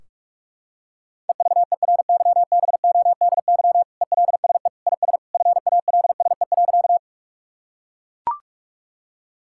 Morse code, heard as a keyed tone, about 42 wpm, 700 Hz. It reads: EVERYBODY ELSE IS FALSE.
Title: Sentences from Top 1600 Words 45wpm